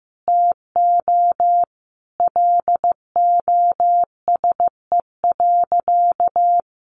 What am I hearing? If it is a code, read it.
Morse code, heard as a keyed tone, about 15 wpm, 700 Hz. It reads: TO LOSE.
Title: Sentences from Top 1100 Words 15wpm